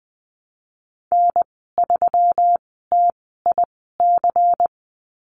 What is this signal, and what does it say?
Morse code, heard as a keyed tone, about 20 wpm, spaced as for 15 wpm, 700 Hz. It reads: N3TIC
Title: 167 Lesson 27g - Number 7 - Call Signs 20wpm